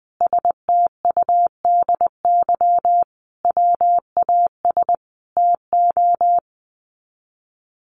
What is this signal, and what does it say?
Morse code, heard as a keyed tone, about 19 wpm, 700 Hz. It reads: STUDY WAS TO
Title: Sets of 5 Words 20wpm